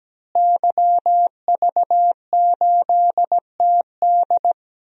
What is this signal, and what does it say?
Morse code, heard as a keyed tone, about 17 wpm, 700 Hz. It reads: YV8TD
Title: DXCC-100 Call Signs - 5 Characters - Repeated 2x - 17wpm